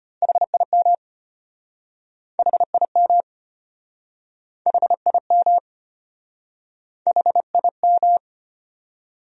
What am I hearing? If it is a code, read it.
Morse code, heard as a keyed tone, about 33 wpm, 700 Hz. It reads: HIM HIM HIM HIM